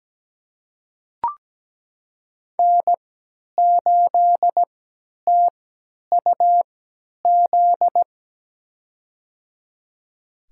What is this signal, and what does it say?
Morse code, heard as a keyed tone, about 17 wpm, spaced as for 10 wpm, 700 Hz. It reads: N8TUZ
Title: USA Call Signs - 5 Characters - 3x Character Spacing - 17wpm